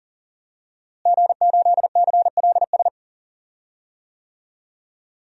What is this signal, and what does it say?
Morse code, heard as a keyed tone, about 39 wpm, 700 Hz. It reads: G8CLS
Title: DXCC-100 Call Signs - 5 Characters - Repeated 3x - 40wpm